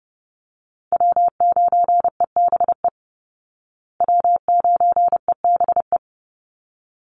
Morse code, heard as a keyed tone, about 30 wpm, 700 Hz. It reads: W9EBE W9EBE
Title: Call Signs Repeated 3x - 30wpm